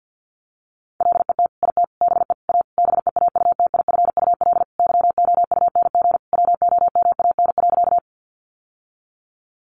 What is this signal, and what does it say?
Morse code, heard as a keyed tone, about 50 wpm, 700 Hz. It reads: LET IT BE A BEAUTIFUL YOUNG WOMAN.
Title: Sentences from Top 1100 Words 50wpm